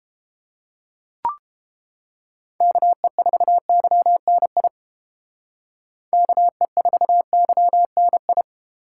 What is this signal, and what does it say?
Morse code, heard as a keyed tone, about 30 wpm, 700 Hz. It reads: KE4YNI KE4YNI